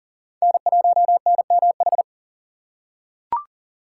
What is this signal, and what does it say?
Morse code, heard as a keyed tone, about 40 wpm, 700 Hz. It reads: N1NMH
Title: USA Call Signs - 5 Characters - 2x Character Spacing - 40wpm